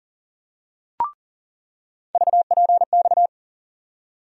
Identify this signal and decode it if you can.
Morse code, heard as a keyed tone, about 40 wpm, 700 Hz. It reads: VPX
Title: USA Call Sign Postfixes - 3 Letters 40wpm